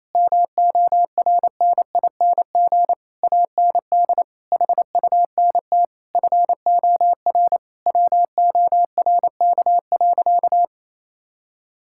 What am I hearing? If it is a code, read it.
Morse code, heard as a keyed tone, about 28 wpm, 700 Hz. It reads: MORNING AND HUNT FOR WORK.
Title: Sentences from Top 1600 Words 28wpm